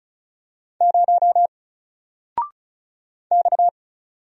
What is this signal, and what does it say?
Morse code, heard as a keyed tone, about 36 wpm, 700 Hz. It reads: 0 X
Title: Single Character (Letter:Number:Punctuation:Prosign) 35wpm